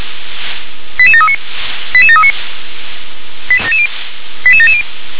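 Five-tone selective call (ZVEI), heard as quick runs of five tones, tones of about 70 ms. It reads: 88739 88739 88770
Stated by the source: ZVEI five-tone call received by SDR.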